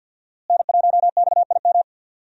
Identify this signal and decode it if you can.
Morse code, heard as a keyed tone, about 53 wpm, 700 Hz. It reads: N1XIM
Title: USA Call Signs - 5 Characters - Repeated 2x - 50wpm